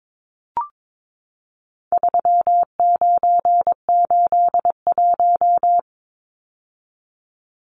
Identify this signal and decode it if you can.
Morse code, heard as a keyed tone, about 22 wpm, 700 Hz. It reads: 3981
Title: Numbers - 4-Digit 22wpm